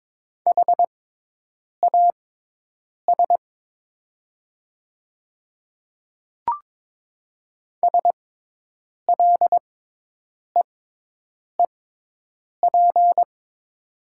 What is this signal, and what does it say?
Morse code, heard as a keyed tone, about 22 wpm, spaced as for 8 wpm, 700 Hz. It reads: HAS SLEEP
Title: 6x Character Spacing - Top 500 Words - 22wpm